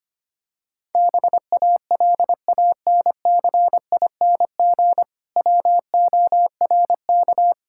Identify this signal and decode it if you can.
Morse code, heard as a keyed tone, about 25 wpm, 700 Hz. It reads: BALANCING WORK
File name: practice / 25wpm / Difficult Sentences with Hints - 25wpm